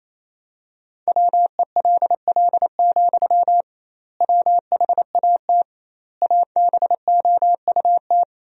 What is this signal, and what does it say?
Morse code, heard as a keyed tone, about 27 wpm, 700 Hz. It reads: WELL, WHAT ABOUT
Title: 2x Word-Spacing - Sentences from Top 500 Words - 28wpm